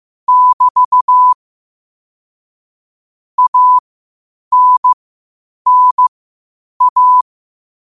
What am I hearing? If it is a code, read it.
Morse code, heard as a keyed tone, about 15 wpm, spaced as for 8 wpm, 1000 Hz. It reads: <BT> ANNA
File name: qotdfiles / 8wpm